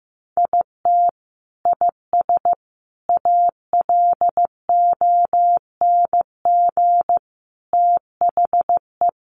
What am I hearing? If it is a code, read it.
Morse code, heard as a keyed tone, about 15 wpm, 700 Hz. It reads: IT IS ALONG THE
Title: Sentences from Top 1200 Words 15wpm